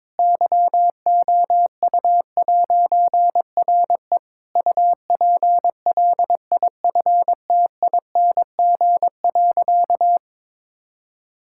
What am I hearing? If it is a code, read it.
Morse code, heard as a keyed tone, about 22 wpm, 700 Hz. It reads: YOU'RE UPLIFTING.
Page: morse-code-ninja-practice-22wpm